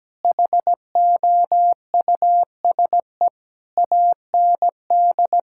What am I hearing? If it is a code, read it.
Morse code, heard as a keyed tone, about 17 wpm, 700 Hz. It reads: HOUSE AND
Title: Sentences from Top 1000 Words - Review Part 5 - 17wpm